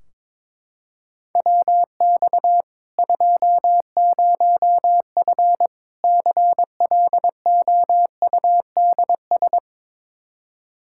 Morse code, heard as a keyed tone, about 22 wpm, 700 Hz. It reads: WX 20F CLOUDS
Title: Practice QSO - Exchange 2 - Info - 22wpm